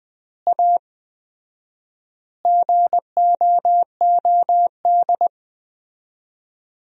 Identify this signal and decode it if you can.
Morse code, heard as a keyed tone, about 21 wpm, 700 Hz. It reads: A GOOD
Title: 4x Word-Spacing - Sentences from Top 400 Words - 20wpm